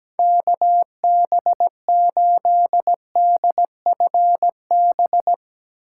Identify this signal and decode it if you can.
Morse code, heard as a keyed tone, about 17 wpm, 700 Hz. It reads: KB8DFB
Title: USA Call Signs - 6 Characters - 8x Character Spacing - 17wpm